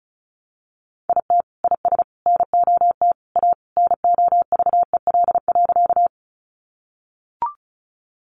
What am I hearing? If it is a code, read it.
Morse code, heard as a keyed tone, about 34 wpm, 700 Hz. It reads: IT IS NOT A NOVEL.
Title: Sentences from Top 1300 Words 35wpm